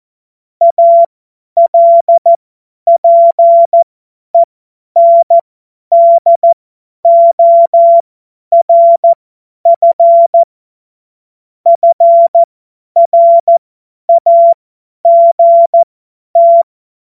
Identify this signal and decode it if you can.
Morse code, heard as a keyed tone, about 14 wpm, spaced as for 10 wpm, 680 Hz. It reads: ALPENDORF FRAGT